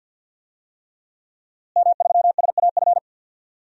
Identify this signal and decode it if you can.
Morse code, heard as a keyed tone, about 52 wpm, 700 Hz. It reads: M3SAF
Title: DXCC-100 Call Signs - 5 Characters - Repeated 3x - 50wpm